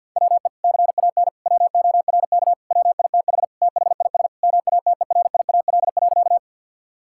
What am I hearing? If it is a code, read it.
Morse code, heard as a keyed tone, about 50 wpm, 700 Hz. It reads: WE CAN WORK WITH THIS MATERIAL.
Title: Sentences from Top 1000 Words - Review Part 5 - 50wpm